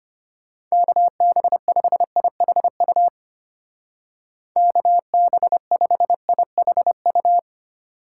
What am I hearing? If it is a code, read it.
Morse code, heard as a keyed tone, about 32 wpm, 700 Hz. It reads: KB5IHU KB5IHU